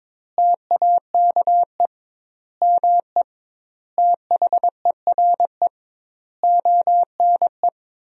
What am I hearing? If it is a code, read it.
Morse code, heard as a keyed tone, about 22 wpm, 700 Hz. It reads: TAKE ME THERE ONE
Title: 2x Word-Spacing - Sentences from Top 300 Words - 22wpm